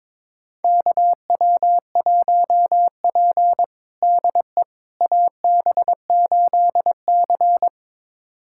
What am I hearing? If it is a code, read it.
Morse code, heard as a keyed tone, about 22 wpm, 700 Hz. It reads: KW1P DE AB8C
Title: USA Callsign DE Callsign - 4 Characters - 22wpm